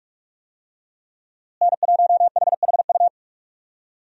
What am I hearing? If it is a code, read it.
Morse code, heard as a keyed tone, about 46 wpm, 700 Hz. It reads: N1HHU
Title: USA Call Signs - 5 Characters - Repeated 2x - 45wpm